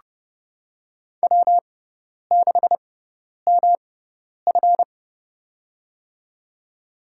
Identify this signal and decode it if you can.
Morse code, heard as a keyed tone, about 31 wpm, spaced as for 10 wpm, 690 Hz. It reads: W6MF
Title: USA Call Signs - 4 Characters - 6x Character Spacing - 30wpm